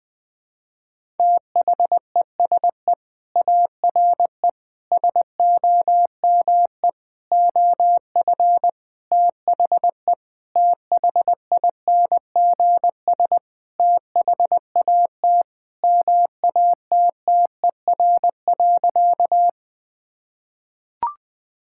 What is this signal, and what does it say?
Morse code, heard as a keyed tone, about 20 wpm, 700 Hz. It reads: THESE ARE SOME OF THE THINGS THAT MATTER.
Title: Sentences with 5-Word Phrases - 20wpm